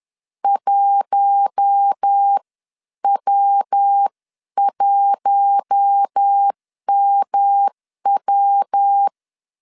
Morse code, heard as a keyed tone, about 10 wpm, spaced as for 9 wpm, 800 Hz. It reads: 1W1MW
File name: CW / DET / 1 6 1500C S C 10wpm 4252